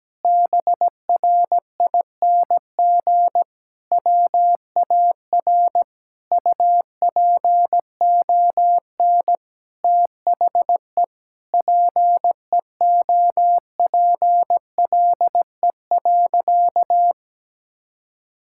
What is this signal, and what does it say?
Morse code, heard as a keyed tone, about 17 wpm, 700 Hz. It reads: BRING WAR UPON THE PEOPLE.
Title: Sentences from Top 1000 Words - Review Part 6 - 17wpm